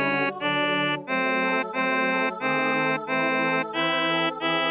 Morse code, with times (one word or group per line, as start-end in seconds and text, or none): none